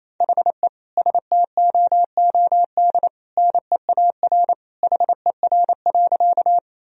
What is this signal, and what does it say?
Morse code, heard as a keyed tone, about 27 wpm, 700 Hz. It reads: HE STOOD NEAR HER.
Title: Sentences from Top 1000 Words - Review Part 2 - 28wpm